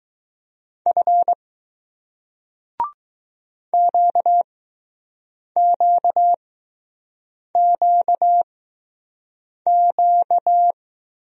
Morse code, heard as a keyed tone, about 20 wpm, spaced as for 6 wpm, 700 Hz. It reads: F QQQQ